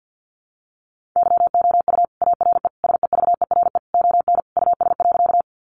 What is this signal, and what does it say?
Morse code, heard as a keyed tone, about 50 wpm, 700 Hz. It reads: YOU ARE SEVERE ON US.
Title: Sentences from Top 1400 Words 50wpm